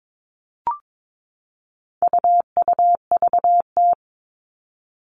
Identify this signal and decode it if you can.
Morse code, heard as a keyed tone, about 22 wpm, 700 Hz. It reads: UUVT
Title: All Cut Numbers - 4-Digit 22wpm